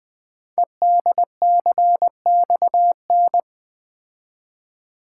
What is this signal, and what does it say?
Morse code, heard as a keyed tone, about 20 wpm, 700 Hz. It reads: EDCXN
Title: Code Groups - 5 Letters - 20wpm